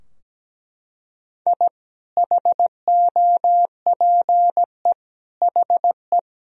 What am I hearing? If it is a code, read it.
Morse code, heard as a keyed tone, about 17 wpm, 700 Hz. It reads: I HOPE HE
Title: Sentences from Top 1000 Words - Review Part 3 - 17wpm